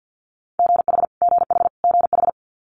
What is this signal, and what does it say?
Morse code, heard as a keyed tone, about 50 wpm, 700 Hz. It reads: GH GH GH
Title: Multi-letter Phonemes Repeated 3x 50wpm